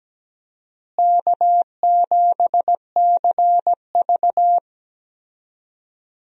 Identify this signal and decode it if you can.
Morse code, heard as a keyed tone, about 17 wpm, 700 Hz. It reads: K7CV
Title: USA Call Signs - 4 Characters - Repeated 3x - 17wpm